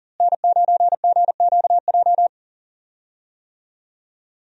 Morse code, heard as a keyed tone, about 40 wpm, 700 Hz. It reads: N9GQJ
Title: USA Call Signs - 5 Characters - Repeated 3x - 40wpm